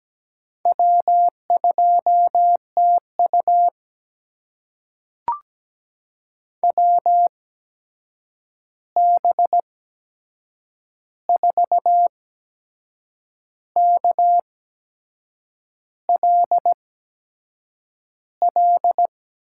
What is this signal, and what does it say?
Morse code, heard as a keyed tone, about 17 wpm, spaced as for 6 wpm, 700 Hz. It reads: W2TU WB4KLL
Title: USA Call Signs - 8x Character Spacing - 17wpm